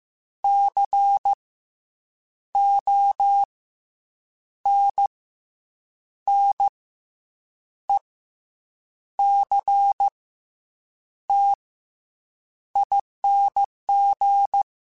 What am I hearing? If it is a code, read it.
Morse code, heard as a keyed tone, about 15 wpm, 800 Hz. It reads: C O N N E C T ING